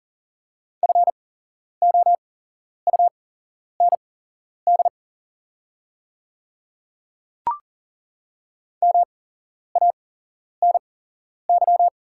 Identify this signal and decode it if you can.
Morse code, heard as a keyed tone, about 40 wpm, spaced as for 11 wpm, 700 Hz. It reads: FOUND MANY
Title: 8x Character Spacing - Top 300 Words - 40wpm